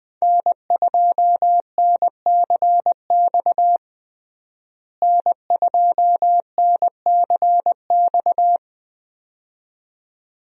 Morse code, heard as a keyed tone, about 20 wpm, 700 Hz. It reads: N2NCX N2NCX
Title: USA Call Signs - 5 Characters - Repeated 2x - 20wpm